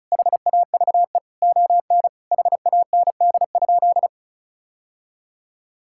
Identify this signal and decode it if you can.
Morse code, heard as a keyed tone, about 36 wpm, 700 Hz. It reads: HAVE ON HAND?
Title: Sentences from Top 1000 Words - Review Part 5 - 35wpm